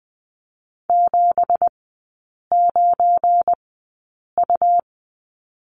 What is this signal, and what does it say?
Morse code, heard as a keyed tone, about 20 wpm, spaced as for 8 wpm, 690 Hz. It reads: 79U
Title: 172 Lesson 28e - Number 9 - Three Characters 20wpm